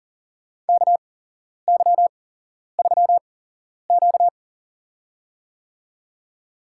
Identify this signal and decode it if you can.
Morse code, heard as a keyed tone, about 41 wpm, spaced as for 11 wpm, 700 Hz. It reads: KY3Q
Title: USA Call Signs - 4 Characters - 8x Character Spacing - 40wpm